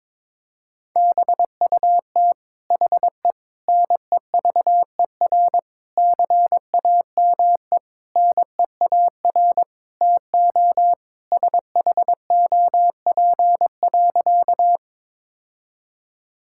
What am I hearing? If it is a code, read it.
Morse code, heard as a keyed tone, about 22 wpm, 700 Hz. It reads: BUT HE NEVER CAME NEAR TO SHOP.
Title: Sentences from Top 1600 Words 22wpm